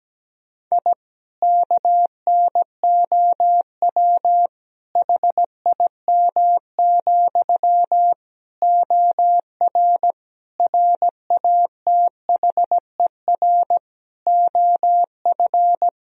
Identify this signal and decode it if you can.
Morse code, heard as a keyed tone, about 17 wpm, 700 Hz. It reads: I KNOW HIM, OR RATHER OF
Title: Sentences from Top 1000 Words - Review Part 4 - 17wpm